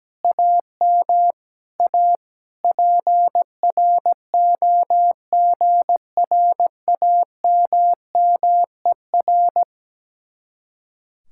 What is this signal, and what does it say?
Morse code, heard as a keyed tone, about 17 wpm, 700 Hz. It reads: AM A PROGRAMMER
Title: Speed-Racing Common QSO Phrases - 17wpm